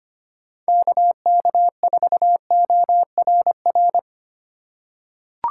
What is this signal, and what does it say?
Morse code, heard as a keyed tone, about 25 wpm, 700 Hz. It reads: KK4ORR
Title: USA Call Signs - 6 Characters - 3x Character Spacing - 25wpm